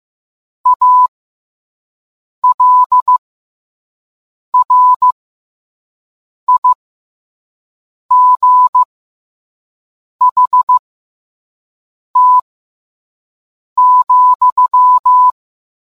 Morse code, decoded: ALRIGHT,

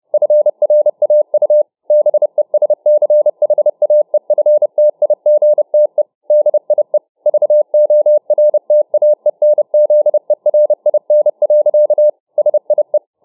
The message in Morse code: FRAU BESCHAEFTIGTE DIE VORTAENZERIN. SIE

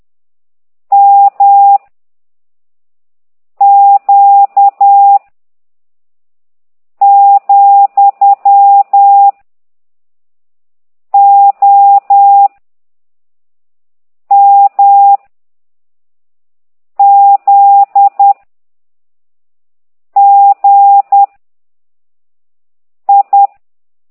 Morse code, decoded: M Q , O M Z G I

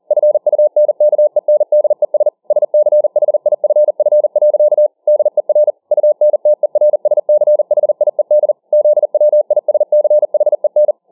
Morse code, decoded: FUNKENDES SCHIFF. DER UNTERSCHIED ZWISCHEN

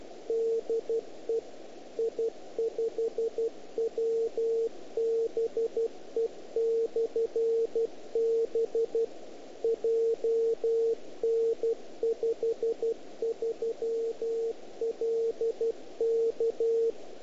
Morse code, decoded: DE I5WBE/B JN53LK

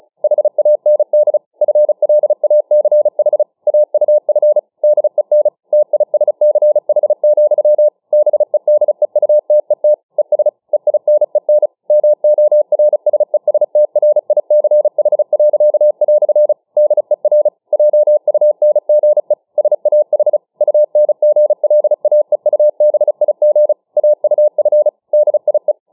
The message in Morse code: HAND FLACH AUF DEN TISCH, BEDEUTET ES EINEN MORSESTRICH." DER JUNGE SAH UNGLAEUBIG AUF DIE

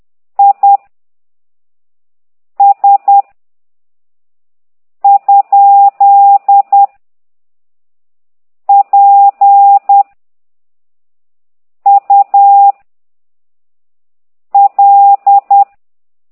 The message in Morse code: I S ? P U L

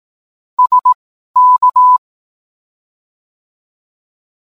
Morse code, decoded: SK

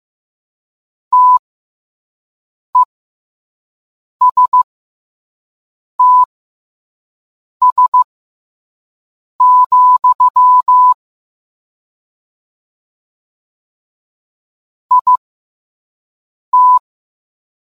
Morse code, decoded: TESTS, IT

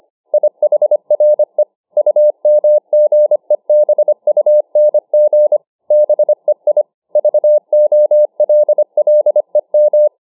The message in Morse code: IHRE UMGEBUNG BEI VOLLEM

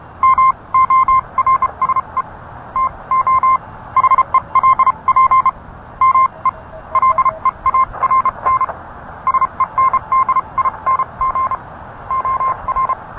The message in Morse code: MORSE TO HELP ME LEARN SENDING OR

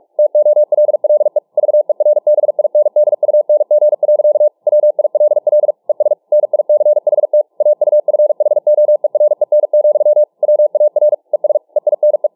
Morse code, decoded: TOLLE VERBINDUNG. WILL ES NICHT AUFHOEREN, WAR ES EINE

